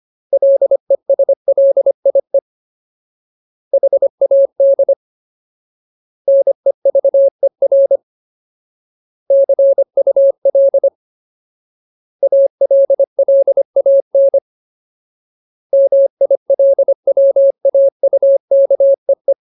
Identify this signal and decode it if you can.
Morse code, heard as a keyed tone, about 25 wpm, 550 Hz. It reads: LESLIE HAD NEVER CUL ALLAN MILWAUKEE
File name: Ultimate Word List / 25WPM Ultimate